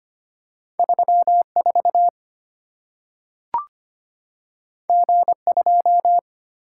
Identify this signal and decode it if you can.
Morse code, heard as a keyed tone, about 25 wpm, 700 Hz. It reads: 34 G2
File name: Two Letter-Number Combinations - 25wpm